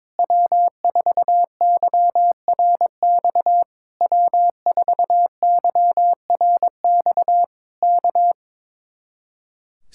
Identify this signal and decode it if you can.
Morse code, heard as a keyed tone, about 22 wpm, 700 Hz. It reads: W4YRX W4YRX K
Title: Speed-Racing Common QSO Phrases Encore - 22wpm